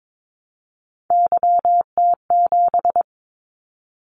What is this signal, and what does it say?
Morse code, heard as a keyed tone, about 22 wpm, 700 Hz. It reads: YT7
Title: Callsign Prefixes for Top 25 DXCC - 22wpm